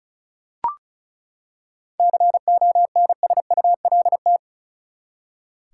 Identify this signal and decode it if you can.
Morse code, heard as a keyed tone, about 36 wpm, 700 Hz. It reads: CONSULT
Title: Seven Letter Words - 35wpm